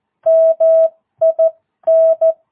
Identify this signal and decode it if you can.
Morse code, heard as a keyed tone, about 14 wpm, spaced as for 12 wpm, 650 Hz. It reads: MIN
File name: jh8cbh / 2022cw